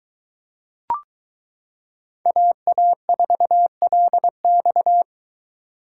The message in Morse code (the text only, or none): AA4LX